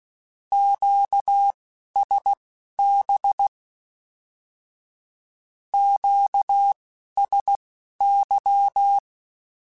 Morse code, heard as a keyed tone, about 16 wpm, spaced as for 11 wpm, 800 Hz. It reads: QSB QSY
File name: Q Calls / Q Calls-mp3s / Q Calls Once @10